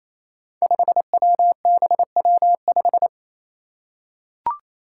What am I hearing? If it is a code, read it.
Morse code, heard as a keyed tone, about 28 wpm, 700 Hz. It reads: 5WBW5
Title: Five Challenging Letter-Number Combinations 28wpm